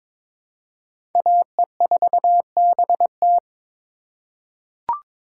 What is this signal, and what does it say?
Morse code, heard as a keyed tone, about 22 wpm, 700 Hz. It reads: AE4BT
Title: USA Call Signs - 5 Characters - 4x Character Spacing - 22wpm